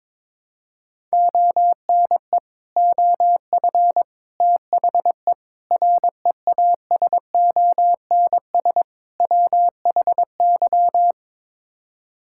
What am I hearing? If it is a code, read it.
Morse code, heard as a keyed tone, about 22 wpm, 700 Hz. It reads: ONE OF THE REASONS WHY
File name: practice / 22wpm / Build A Phrase - 5-Word Phrases - 22wpm